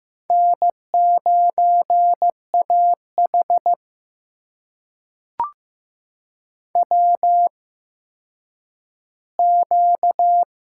Morse code, heard as a keyed tone, about 15 wpm, spaced as for 9 wpm, 700 Hz. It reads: N9AH WQ
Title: USA Call Signs - 4 Characters - 8x Character Spacing - 15wpm